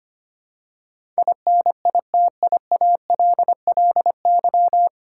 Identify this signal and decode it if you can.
Morse code, heard as a keyed tone, about 25 wpm, 700 Hz. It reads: INITIALLY
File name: Nine Letter Words - 25wpm